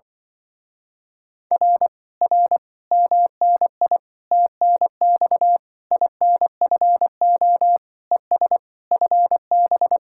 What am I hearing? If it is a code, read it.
Morse code, heard as a keyed tone, about 24 wpm, 700 Hz. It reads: R R MNI TNX INFO ES FB
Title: Speed-Racing Common QSO Phrases Encore - 20wpm